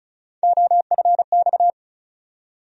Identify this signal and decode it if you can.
Morse code, heard as a keyed tone, about 36 wpm, 700 Hz. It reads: OFX
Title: USA Call Sign Postfixes - 3 Letters 35wpm